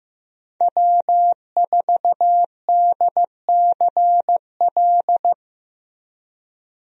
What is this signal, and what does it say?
Morse code, heard as a keyed tone, about 15 wpm, 700 Hz. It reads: W4DCL
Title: USA Call Signs - 5 Characters - Repeated 2x - 15wpm